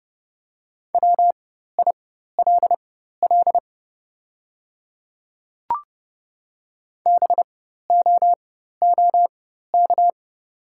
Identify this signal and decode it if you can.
Morse code, heard as a keyed tone, about 30 wpm, spaced as for 14 wpm, 700 Hz. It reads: WILL BOOK
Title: 4x Character Spacing - Top 300 Words - 30wpm